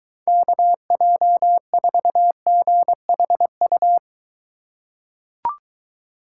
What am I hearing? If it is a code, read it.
Morse code, heard as a keyed tone, about 23 wpm, 700 Hz. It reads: KJ4GHU